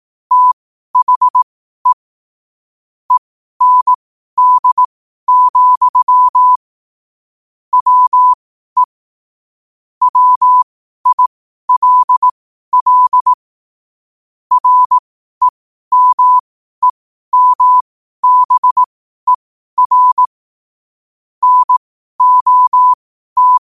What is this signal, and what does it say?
Morse code, heard as a keyed tone, about 18 wpm, spaced as for 12 wpm, 1000 Hz. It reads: THE END, WE WILL REMEMBER NOT
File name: qotdfiles / 12wpm